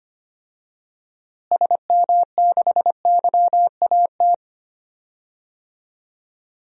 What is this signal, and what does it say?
Morse code, heard as a keyed tone, about 25 wpm, 700 Hz. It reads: SM6YAT
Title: DXCC-100 Call Signs - 6 Characters - Repeated 2x - 25wpm